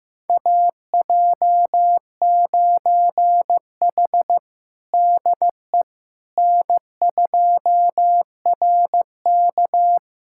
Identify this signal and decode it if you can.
Morse code, heard as a keyed tone, about 15 wpm, 700 Hz. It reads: AJ9H DE N2RK
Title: USA Callsign DE Callsign - 4 Characters - 15wpm